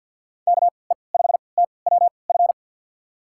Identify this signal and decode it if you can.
Morse code, heard as a keyed tone, about 48 wpm, spaced as for 28 wpm, 700 Hz. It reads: KE5TWF
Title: USA Call Signs - 6 Characters - 3x Character Spacing - 50wpm